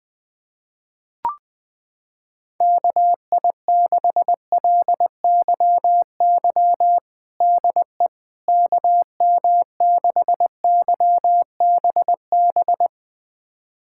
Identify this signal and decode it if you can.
Morse code, heard as a keyed tone, about 20 wpm, 700 Hz. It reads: KI6LYY DE KM6YBB
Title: USA Callsign DE Callsign - 6 Characters - 20wpm